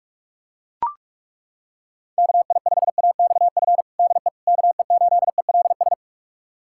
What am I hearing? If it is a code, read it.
Morse code, heard as a keyed tone, about 42 wpm, 700 Hz. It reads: KI5AXF DE KE8ELS
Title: USA Callsign DE Callsign - 6 Characters - 45wpm